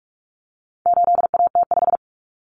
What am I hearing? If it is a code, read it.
Morse code, heard as a keyed tone, about 44 wpm, 700 Hz. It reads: 8AT5
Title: Common Cut Numbers - 4-Digit 45wpm